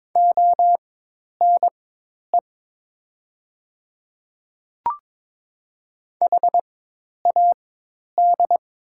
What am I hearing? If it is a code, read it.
Morse code, heard as a keyed tone, about 22 wpm, spaced as for 10 wpm, 690 Hz. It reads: ONE HAD
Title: 4x Character Spacing - Top 100 Words - 22wpm